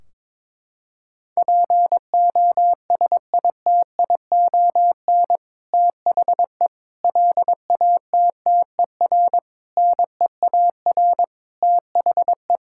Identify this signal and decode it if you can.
Morse code, heard as a keyed tone, about 22 wpm, 700 Hz. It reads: POSITION THE LATTER NEAR THE
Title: Sentences from Top 1600 Words 22wpm